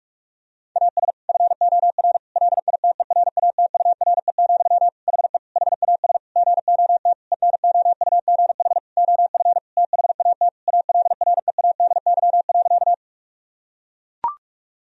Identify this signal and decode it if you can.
Morse code, heard as a keyed tone, about 41 wpm, 700 Hz. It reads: AS FOR LITERATURE, HE HAS GOT ENOUGH OF THAT ALREADY.
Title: Sentences from Top 1000 Words - Review Part 3 - 45wpm